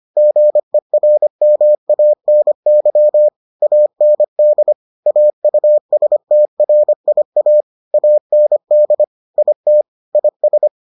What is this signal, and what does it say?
Morse code, heard as a keyed tone, about 25 wpm, 600 Hz. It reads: GERMANY AND AUSTRIA AND IT IS